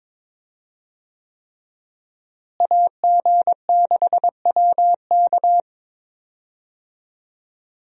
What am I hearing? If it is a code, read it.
Morse code, heard as a keyed tone, about 22 wpm, 700 Hz. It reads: AG6WK
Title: USA Call Signs - 5 Characters - Repeated 3x - 22wpm